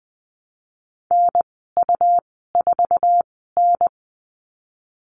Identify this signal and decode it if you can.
Morse code, heard as a keyed tone, about 20 wpm, spaced as for 15 wpm, 700 Hz. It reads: NU4N
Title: 167 Lesson 27g - Number 7 - Call Signs 20wpm